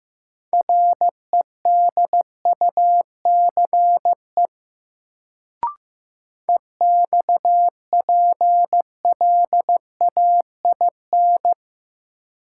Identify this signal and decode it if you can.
Morse code, heard as a keyed tone, about 16 wpm, 700 Hz. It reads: REDUCE EXPLAIN
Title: Sentences from Top 1100 Words - New Word Drill Alt - 15wpm